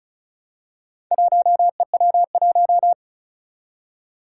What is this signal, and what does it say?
Morse code, heard as a keyed tone, about 36 wpm, 700 Hz. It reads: 1EW1
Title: Four Challenging Letter-Number Combinations 35wpm